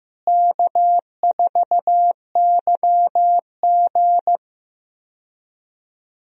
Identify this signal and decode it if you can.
Morse code, heard as a keyed tone, about 15 wpm, 700 Hz. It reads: K4YG
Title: USA Call Signs - 4 Characters - Repeated 2x - 15wpm